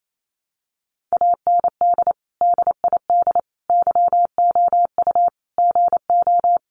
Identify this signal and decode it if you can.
Morse code, heard as a keyed tone, about 28 wpm, 700 Hz. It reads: AND DID YOU GO
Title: Sentences from Top 1000 Words 28wpm